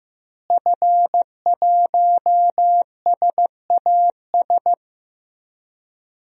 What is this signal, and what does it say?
Morse code, heard as a keyed tone, about 15 wpm, 700 Hz. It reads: F1SAS
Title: DXCC-100 Call Signs - 5 Characters - Repeated 3x - 15wpm